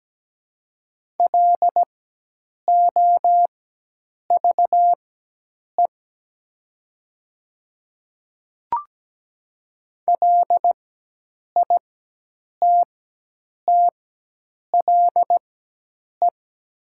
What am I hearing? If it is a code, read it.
Morse code, heard as a keyed tone, about 17 wpm, spaced as for 8 wpm, 700 Hz. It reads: LOVE LITTLE
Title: 4x Character Spacing - Top 500 Words - 17wpm